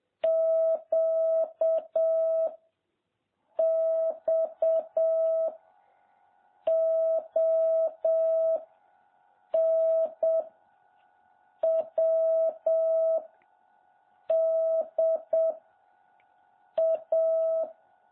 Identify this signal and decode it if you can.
Morse code, heard as a keyed tone, about 7 wpm, 650 Hz. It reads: Q X O N W D A
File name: jh8cbh / 2021cw